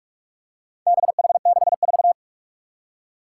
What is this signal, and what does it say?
Morse code, heard as a keyed tone, about 41 wpm, 700 Hz. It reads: BH64
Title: Four Challenging Letter-Number Combinations 45wpm